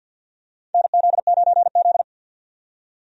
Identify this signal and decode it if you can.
Morse code, heard as a keyed tone, about 53 wpm, 700 Hz. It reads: NZ9Z
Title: USA Call Signs - 4 Characters - 50wpm